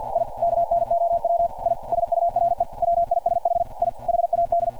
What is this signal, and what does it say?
Morse code, heard as a keyed tone, about 52 wpm, 690 Hz. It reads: I KNOW I SURE WISH I HAD
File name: archived / CRSnet